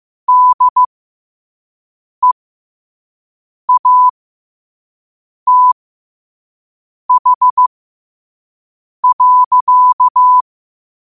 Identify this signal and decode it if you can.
Morse code, heard as a keyed tone, about 15 wpm, spaced as for 5 wpm, 1000 Hz. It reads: DEATH.
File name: qotdfiles / 5wpm